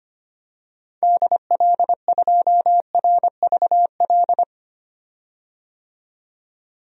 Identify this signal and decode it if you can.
Morse code, heard as a keyed tone, about 25 wpm, 700 Hz. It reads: DL2RVL
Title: DXCC-100 Call Signs - 6 Characters - 25wpm